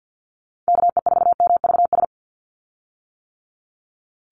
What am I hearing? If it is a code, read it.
Morse code, heard as a keyed tone, about 50 wpm, 700 Hz. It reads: KE4MVS
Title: USA Call Signs 50wpm